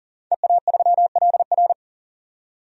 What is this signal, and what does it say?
Morse code, heard as a keyed tone, about 40 wpm, 700 Hz. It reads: EA3LR
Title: DXCC-100 Call Signs - 5 Characters - 40wpm